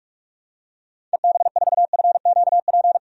Morse code, heard as a keyed tone, about 44 wpm, 700 Hz. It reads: EB4FQP
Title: DXCC-100 Call Signs - 6 Characters - 45wpm